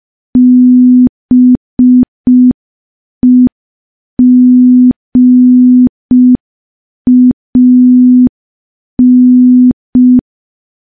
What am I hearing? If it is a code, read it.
Morse code, heard as a keyed tone, about 5 wpm, 250 Hz. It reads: BEGAN
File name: morse / morse-training-05WPM